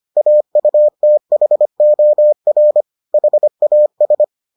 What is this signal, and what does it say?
Morse code, heard as a keyed tone, about 25 wpm, 600 Hz. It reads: AUTHOR HAS